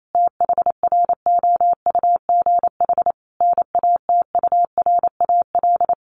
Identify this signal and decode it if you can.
Morse code, heard as a keyed tone, about 28 wpm, 700 Hz. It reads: THROUGH NATURAL
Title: Sentences from Top 1400 Words 28wpm